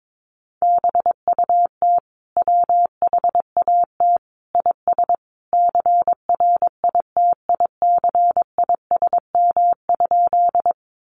Morse code, heard as a keyed tone, about 22 wpm, 700 Hz. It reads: BUT WHAT IS CRITICISM?
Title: Sentences from Top 1300 Words 22wpm